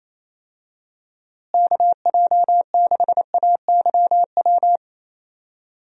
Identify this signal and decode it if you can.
Morse code, heard as a keyed tone, about 28 wpm, 700 Hz. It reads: KJ6AYW
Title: USA Call Signs - 6 Characters - Repeated 3x - 28wpm